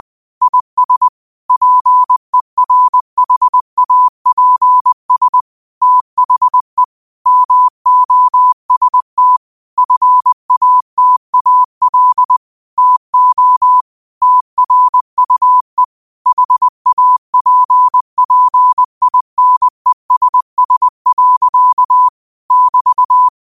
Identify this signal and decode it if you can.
Morse code, heard as a keyed tone, about 20 wpm, 1000 Hz. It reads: IS PERHAPS THE MOST FATAL TO TRUE HAPPINESS. <BT>
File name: qotdfiles / 20wpm